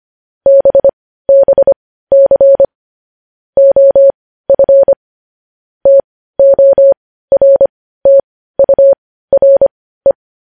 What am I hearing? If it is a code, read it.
Morse code, heard as a keyed tone, about 25 wpm, spaced as for 15 wpm, 550 Hz. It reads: BBC OF TORTURE
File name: podcast / News headlines-1x-25x15